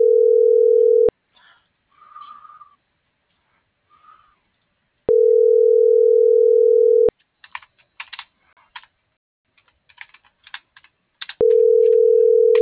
Ringback tone (North America), 440 + 480 Hz.